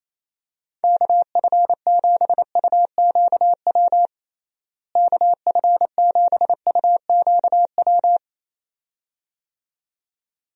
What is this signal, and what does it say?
Morse code, heard as a keyed tone, about 28 wpm, 700 Hz. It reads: KF7UQW KF7UQW
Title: USA Call Signs - 6 Characters - Repeated 2x - 28wpm